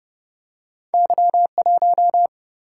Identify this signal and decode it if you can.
Morse code, heard as a keyed tone, about 32 wpm, 700 Hz. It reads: Y 1
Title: Single-Letter Single-Number 30wpm